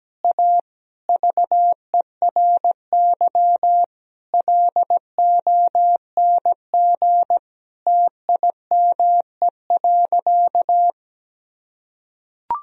Morse code, heard as a keyed tone, about 17 wpm, 700 Hz. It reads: A VERY LONG TIME.
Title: Sentences from Top 1000 Words - Review Part 3 - 17wpm